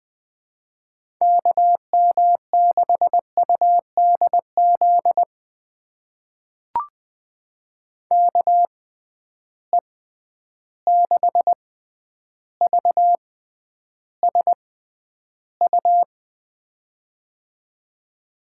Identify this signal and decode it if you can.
Morse code, heard as a keyed tone, about 20 wpm, spaced as for 10 wpm, 700 Hz. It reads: KM6UDZ KE6VSU
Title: USA Call Signs - 6x Character Spacing - 20wpm